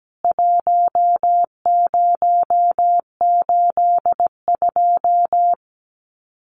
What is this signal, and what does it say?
Morse code, heard as a keyed tone, about 17 wpm, 700 Hz. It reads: 1082
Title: CWT Exchange 17wpm